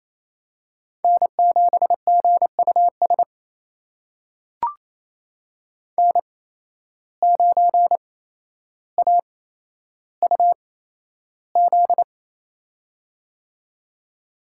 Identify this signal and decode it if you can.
Morse code, heard as a keyed tone, about 28 wpm, spaced as for 12 wpm, 700 Hz. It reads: N7GUS N9AUZ
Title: USA Call Signs - 5 Characters - 8x Character Spacing - 28wpm